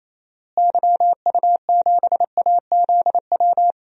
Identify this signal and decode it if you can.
Morse code, heard as a keyed tone, about 28 wpm, 700 Hz. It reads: YU7AZW
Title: DXCC-100 Call Signs - 6 Characters - Repeated 3x - 28wpm